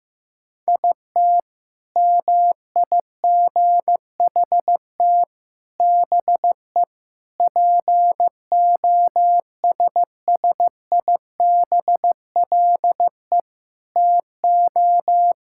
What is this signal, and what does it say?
Morse code, heard as a keyed tone, about 15 wpm, 700 Hz. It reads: IT MIGHT BE POSSIBLE TO
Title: Sentences from Top 1000 Words - Review Part 6 - 15wpm